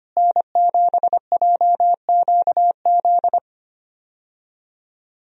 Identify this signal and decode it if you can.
Morse code, heard as a keyed tone, about 25 wpm, 700 Hz. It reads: N7JQZ
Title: USA Call Signs - 5 Characters - Repeated 3x - 25wpm